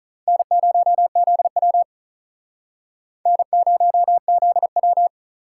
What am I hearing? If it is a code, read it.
Morse code, heard as a keyed tone, about 47 wpm, 700 Hz. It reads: N0ZW N0ZW